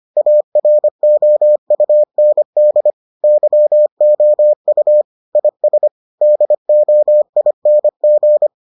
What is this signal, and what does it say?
Morse code, heard as a keyed tone, about 25 wpm, 600 Hz. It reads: AROUND YOU IS DOING